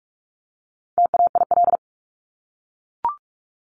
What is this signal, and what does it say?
Morse code, heard as a keyed tone, about 44 wpm, 700 Hz. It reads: TAIL
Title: Top 500 Words Repeated 45wpm